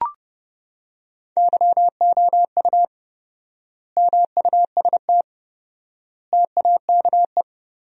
Morse code, heard as a keyed tone, about 30 wpm, 700 Hz. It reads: YOU MUST TAKE
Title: 4x Word-Spacing - Sentences from Top 200 Words - 30wpm